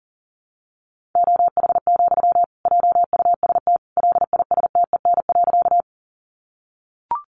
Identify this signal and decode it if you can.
Morse code, heard as a keyed tone, about 40 wpm, 700 Hz. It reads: OH, JUST LISTEN.
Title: Sentences from Top 500 Words 40wpm